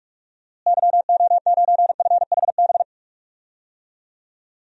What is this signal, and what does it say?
Morse code, heard as a keyed tone, about 44 wpm, 700 Hz. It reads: YO9FHB
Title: DXCC-100 Call Signs - 6 Characters - Repeated 2x - 45wpm